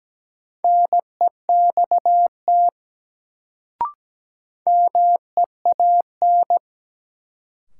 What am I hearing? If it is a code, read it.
Morse code, heard as a keyed tone, about 17 wpm, 700 Hz. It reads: NEXT MEAN
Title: Sets of 2 Words - New Word Drill - 17wpm